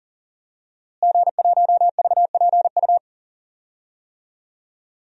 Morse code, heard as a keyed tone, about 37 wpm, 700 Hz. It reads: G1VPU
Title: DXCC-100 Call Signs - 5 Characters - Repeated 3x - 40wpm